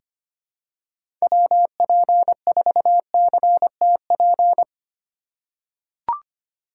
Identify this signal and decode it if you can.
Morse code, heard as a keyed tone, about 25 wpm, 700 Hz. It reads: WP4CTP
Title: USA Call Signs - 6 Characters - 3x Character Spacing - 25wpm